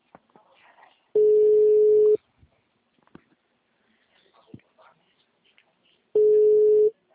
Ringback tone (Europe, 425 Hz).
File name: out- -2040- - - .29738